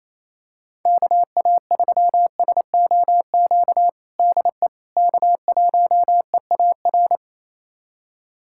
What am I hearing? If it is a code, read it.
Morse code, heard as a keyed tone, about 28 wpm, 700 Hz. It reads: KA3SOQ DE K1EAR
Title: USA Callsign DE Callsign 28wpm